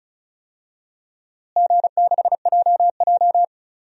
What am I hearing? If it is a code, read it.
Morse code, heard as a keyed tone, about 36 wpm, 700 Hz. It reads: G6JJ
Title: DXCC-100 Call Signs - 4 Characters - Repeated 2x - 35wpm